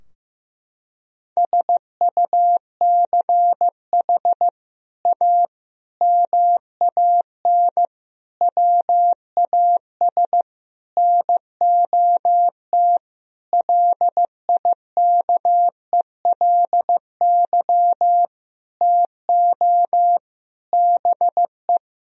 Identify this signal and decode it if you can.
Morse code, heard as a keyed tone, about 16 wpm, 700 Hz. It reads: SUCH A MAN WAS NOT LIKELY TO BE
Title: Sentences from Top 1000 Words - Review Part 4 - 15wpm